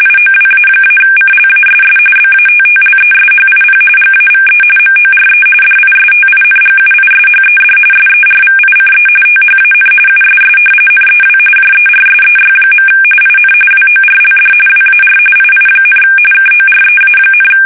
RTTY, about 45.45 baud. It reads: IOWA-CLASS BATTLESHIP GAVE MORE DISTINGUISHED SERVICE THAN USS NEW JERSEY. IN WORLD WAR II, KOREA,
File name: fsk850 / battleshipnj